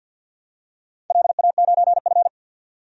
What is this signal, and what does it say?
Morse code, heard as a keyed tone, about 53 wpm, 700 Hz. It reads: LA9F
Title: DXCC-100 Call Signs - 4 Characters - 50wpm